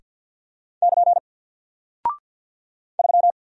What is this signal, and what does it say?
Morse code, heard as a keyed tone, about 48 wpm, 700 Hz. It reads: <KN> 3
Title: Single Character (Letter:Number:Punctuation:Prosign) Alt 50wpm